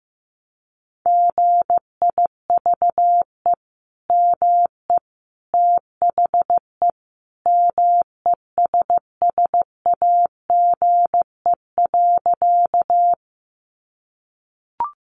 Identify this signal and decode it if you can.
Morse code, heard as a keyed tone, about 15 wpm, 700 Hz. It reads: GIVE ME THE MESSAGE.
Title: Sentences from Top 1100 Words 15wpm